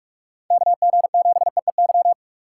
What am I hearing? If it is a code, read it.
Morse code, heard as a keyed tone, about 46 wpm, 700 Hz. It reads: KG7EEY
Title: USA Call Signs - 3x Character Spacing - 45wpm